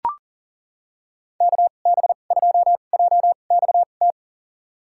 Morse code, heard as a keyed tone, about 37 wpm, spaced as for 28 wpm, 700 Hz. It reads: KB2JXT
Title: USA Call Signs - 2x Character Spacing - 40wpm